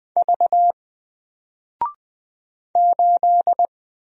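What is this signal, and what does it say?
Morse code, heard as a keyed tone, about 20 wpm, 700 Hz. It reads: V 8
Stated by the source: Single letter and number Morse drill at 20 wpm, from N4LCD.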